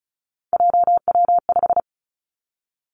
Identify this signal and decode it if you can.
Morse code, heard as a keyed tone, about 36 wpm, 700 Hz. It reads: JW5
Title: Three Challenging Letter-Number Combinations 35wpm